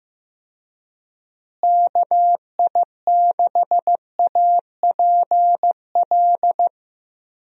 Morse code, heard as a keyed tone, about 15 wpm, 700 Hz. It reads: KI6APL
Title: USA Call Signs - 6 Characters - Repeated 3x - 15wpm